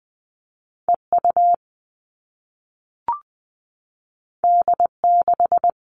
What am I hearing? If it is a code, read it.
Morse code, heard as a keyed tone, about 20 wpm, 700 Hz. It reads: EU D6
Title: All Cut Numbers - 2-Digit 20wpm